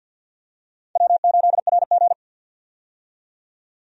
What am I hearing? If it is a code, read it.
Morse code, heard as a keyed tone, about 53 wpm, 700 Hz. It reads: W8RG
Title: USA Call Signs - 4 Characters - 50wpm